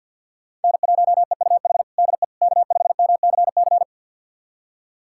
Morse code, heard as a keyed tone, about 53 wpm, 700 Hz. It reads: N1EUH DE K5MCC
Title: USA Callsign DE Callsign - 5 Characters - 50wpm